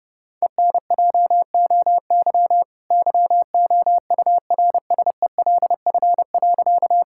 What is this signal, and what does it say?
Morse code, heard as a keyed tone, about 30 wpm, 700 Hz. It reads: ENJOY YOURSELF.
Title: Sentences from Top 1000 Words - Review Part 5 - 30wpm